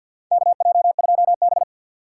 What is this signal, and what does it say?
Morse code, heard as a keyed tone, about 47 wpm, 700 Hz. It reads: KJ2B
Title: USA Call Signs - 3x Character Spacing - 50wpm